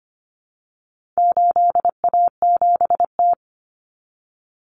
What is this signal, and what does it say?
Morse code, heard as a keyed tone, about 25 wpm, 700 Hz. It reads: 8A7T
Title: Common Cut Numbers - 4-Digit 25wpm